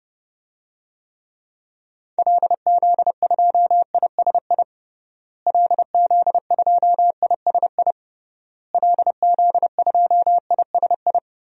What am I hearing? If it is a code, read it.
Morse code, heard as a keyed tone, about 30 wpm, 700 Hz. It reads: LZ2ISI LZ2ISI LZ2ISI
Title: DXCC-100 Call Signs - 6 Characters - Repeated 3x - 30wpm